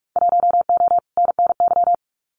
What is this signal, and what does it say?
Morse code, heard as a keyed tone, about 46 wpm, 700 Hz. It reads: 1O NNY
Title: Winter Field Day Exchanges 45wpm